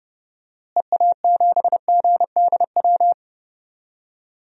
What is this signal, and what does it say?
Morse code, heard as a keyed tone, about 30 wpm, 700 Hz. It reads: EA7GDW
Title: DXCC-100 Call Signs - 6 Characters - Repeated 2x - 30wpm